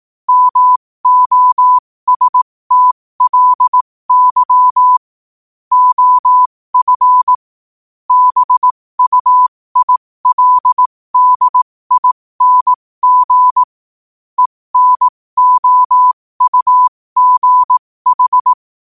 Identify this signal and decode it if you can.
Morse code, heard as a keyed tone, about 18 wpm, spaced as for 15 wpm, 1000 Hz. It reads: MOSTLY OF BUILDING ENOUGH
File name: qotdfiles / 15wpm